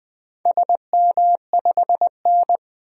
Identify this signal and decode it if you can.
Morse code, heard as a keyed tone, about 20 wpm, 700 Hz. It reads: SM5N
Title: DXCC-100 Call Signs - 4 Characters - 20wpm